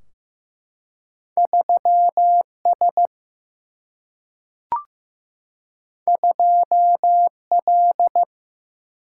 Morse code, heard as a keyed tone, about 15 wpm, 700 Hz. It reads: 3S 2L
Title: Single-Letter Single-Number 15wpm